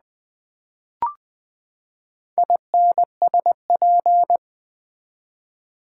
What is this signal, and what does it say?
Morse code, heard as a keyed tone, about 20 wpm, 700 Hz. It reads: INSP
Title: Build A Word - 4 Letters At A Time - 11-Letter Words - 20wpm